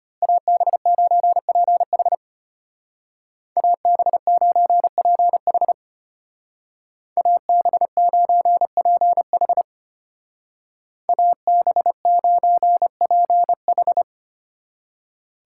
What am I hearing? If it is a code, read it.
Morse code, heard as a keyed tone, about 30 wpm, 700 Hz. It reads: AB9PH AB9PH AB9PH AB9PH
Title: Speed Racing - USA 2x2 Call Signs 25wpm